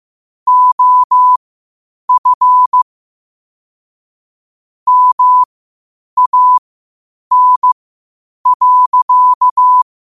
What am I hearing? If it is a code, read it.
Morse code, heard as a keyed tone, about 15 wpm, spaced as for 8 wpm, 1000 Hz. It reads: OF MAN.
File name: qotdfiles / 8wpm